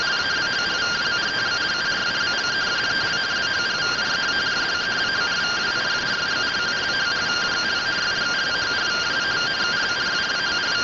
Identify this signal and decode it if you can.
Two-tone teleprinter signal, about 170 Hz shift.